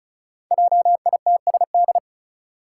Morse code, heard as a keyed tone, about 34 wpm, 700 Hz. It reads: JITSD
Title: Five Challenging Letter-Number Combinations 35wpm